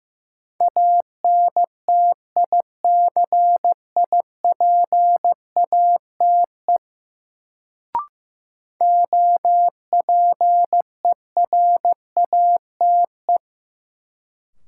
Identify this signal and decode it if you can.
Morse code, heard as a keyed tone, about 15 wpm, 700 Hz. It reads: ANTICIPATE OPERATE
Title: Sentences from Top 1500 Words - New Word Drill Alt - 15wpm